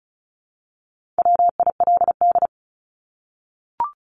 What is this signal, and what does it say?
Morse code, heard as a keyed tone, about 34 wpm, 700 Hz. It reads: WILD
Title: Top 1501-1750 Words 35wpm